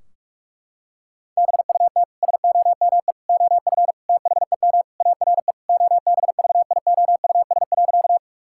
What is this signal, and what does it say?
Morse code, heard as a keyed tone, about 46 wpm, 700 Hz. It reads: BUT SOME OF THEM ARE OBVIOUS.
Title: Sentences from Top 1600 Words 45wpm